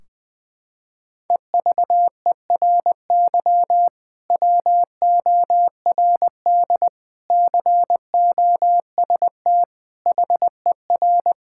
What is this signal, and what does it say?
Morse code, heard as a keyed tone, about 20 wpm, 700 Hz. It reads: EVERY WORD COST HER
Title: Sentences from Top 1600 Words 20wpm